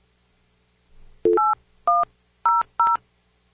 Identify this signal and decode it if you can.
DTMF, heard as telephone touch tones, tones of about 190 ms. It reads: 8100